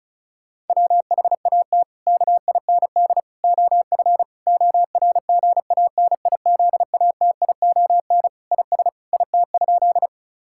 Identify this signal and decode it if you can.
Morse code, heard as a keyed tone, about 34 wpm, 700 Hz. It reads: WHAT KIND OF ORGANIZATION IS IT?
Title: Sentences from Top 1000 Words - Review Part 5 - 35wpm